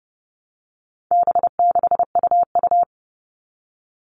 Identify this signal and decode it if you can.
Morse code, heard as a keyed tone, about 30 wpm, 700 Hz. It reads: B6UU
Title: All Cut Numbers - 4-Digit 30wpm